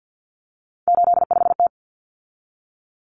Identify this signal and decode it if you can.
Morse code, heard as a keyed tone, about 50 wpm, 700 Hz. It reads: 85T
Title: Common Cut Numbers - 3-Digit 50wpm